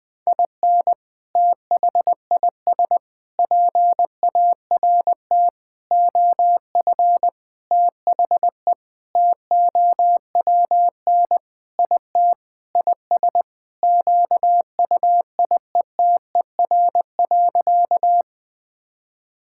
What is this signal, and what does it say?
Morse code, heard as a keyed tone, about 19 wpm, 700 Hz. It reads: IN THIS PART OF THE TOWN IT IS QUIETER.
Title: Sentences with 5-Word Phrases - 20wpm